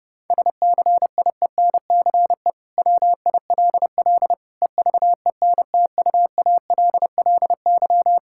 Morse code, heard as a keyed tone, about 30 wpm, 700 Hz. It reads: SCIENCE WILL EVENTUALLY
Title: Sentences from Top 1000 Words - Review Part 6 - 30wpm